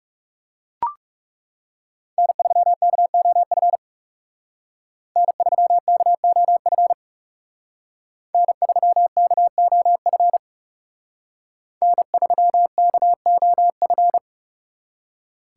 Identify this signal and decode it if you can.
Morse code, heard as a keyed tone, about 34 wpm, 700 Hz. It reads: N3KOF N3KOF N3KOF N3KOF
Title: Speed Racing - USA 1x3 Call Signs 30wpm